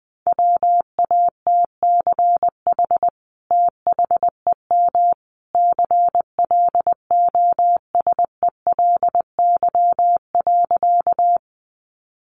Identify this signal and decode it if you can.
Morse code, heard as a keyed tone, about 20 wpm, 700 Hz. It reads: WATCH THEM CLOSELY.
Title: Sentences from Top 1400 Words 20wpm